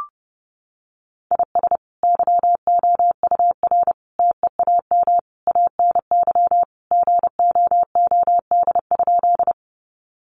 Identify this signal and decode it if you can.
Morse code, heard as a keyed tone, about 30 wpm, 700 Hz. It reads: IS YOUR TEAM ANY GOOD?
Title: Sentences from Top 1500 Words 30wpm